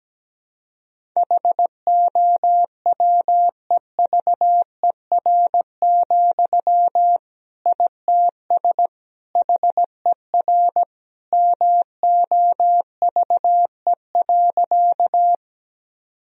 Morse code, heard as a keyed tone, about 17 wpm, 700 Hz. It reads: HOWEVER, ITS HER MOVE.
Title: Sentences from Top 1000 Words - Review Part 1 - 17wpm